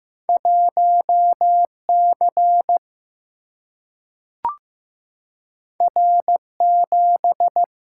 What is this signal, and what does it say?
Morse code, heard as a keyed tone, about 15 wpm, 700 Hz. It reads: 1 C R 7
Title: Single-Letter Single-Number 15wpm